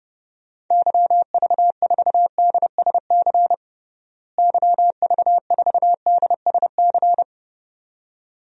Morse code, heard as a keyed tone, about 30 wpm, 700 Hz. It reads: YV4DSC YV4DSC
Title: DXCC-100 Call Signs - 6 Characters - Repeated 2x - 30wpm